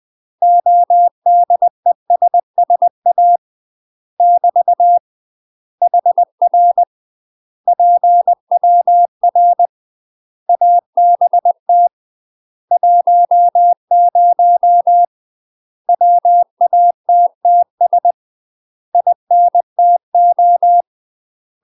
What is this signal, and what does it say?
Morse code, heard as a keyed tone, about 20 wpm, 700 Hz. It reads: ODESSA <BT> HR PWR ABT 10 WATTS INTO
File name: UK MASTER QSOs TRAINING / 20WPM / UK MASTER QSOs 0020